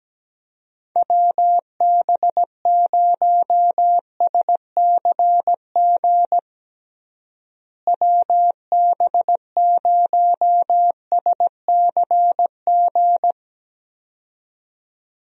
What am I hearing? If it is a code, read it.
Morse code, heard as a keyed tone, about 17 wpm, 700 Hz. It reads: WB0SCG WB0SCG
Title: USA Call Signs - 6 Characters - Repeated 2x - 17wpm